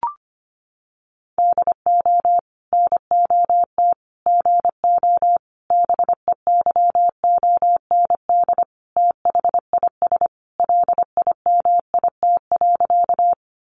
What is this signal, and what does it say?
Morse code, heard as a keyed tone, about 25 wpm, 700 Hz. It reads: DO NOT GO BEYOND THIS LIMIT.
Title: Sentences from Top 1500 Words 25wpm